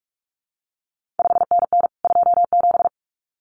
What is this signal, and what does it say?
Morse code, heard as a keyed tone, about 46 wpm, 700 Hz. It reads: 5NN 27
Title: CQ WW DX Contest Exchange Encore - 45wpm